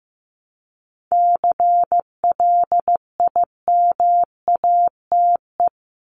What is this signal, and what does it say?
Morse code, heard as a keyed tone, about 15 wpm, 700 Hz. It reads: CLIMATE
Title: Top 1501-1750 Words 15wpm